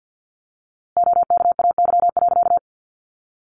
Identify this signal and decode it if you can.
Morse code, heard as a keyed tone, about 50 wpm, 700 Hz. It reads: OKAY.
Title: A Sound Of Thunder 50wpm